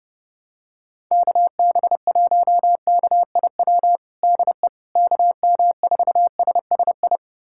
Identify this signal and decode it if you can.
Morse code, heard as a keyed tone, about 30 wpm, 700 Hz. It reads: KB1KIW DE KM4SSI
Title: USA Callsign DE Callsign 30wpm